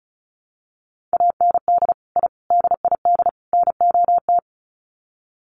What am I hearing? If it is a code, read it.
Morse code, heard as a keyed tone, about 34 wpm, 700 Hz. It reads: AND I DID NOT
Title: Sets of 4 Words 35wpm